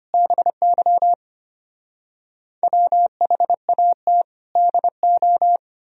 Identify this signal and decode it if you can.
Morse code, heard as a keyed tone, about 33 wpm, 700 Hz. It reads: BY WHAT DO